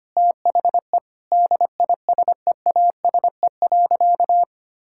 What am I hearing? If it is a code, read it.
Morse code, heard as a keyed tone, about 25 wpm, 700 Hz. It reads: THE DISEASE.
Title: Sentences from Top 1000 Words - Review Part 6 - 25wpm